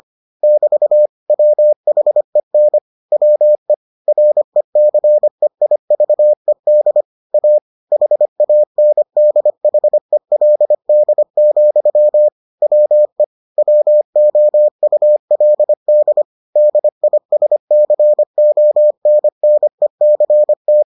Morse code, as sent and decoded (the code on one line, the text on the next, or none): -...- / .-- .... -. / .-- . / .-. -.-. .. ...- . -.. / .- / .... .- -. -.. .... .-.. -.. --..-- / .-- . / .-- --- ..- .-.. -.. / -.. .. ... -.-. --- -. -. -.-. -
<BT> WHEN WE RECEIVED A HANDHELD, WE WOULD DISCONNECT